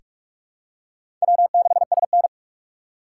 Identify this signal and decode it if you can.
Morse code, heard as a keyed tone, about 43 wpm, 690 Hz. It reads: W6SN
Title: USA Call Signs - 4 Characters - 2x Character Spacing - 45wpm